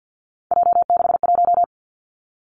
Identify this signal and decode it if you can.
Morse code, heard as a keyed tone, about 50 wpm, 700 Hz. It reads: J61
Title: Three Challenging Letter-Number Combinations 50wpm